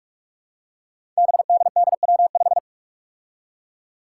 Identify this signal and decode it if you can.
Morse code, heard as a keyed tone, about 46 wpm, 700 Hz. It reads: BDDW5
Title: Five Challenging Letter-Number Combinations 45wpm